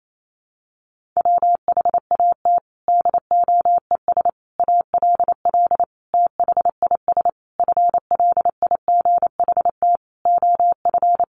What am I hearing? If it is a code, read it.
Morse code, heard as a keyed tone, about 28 wpm, 700 Hz. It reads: WHAT DOES ALL THIS FLIGHT OF
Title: Sentences from Top 1300 Words 28wpm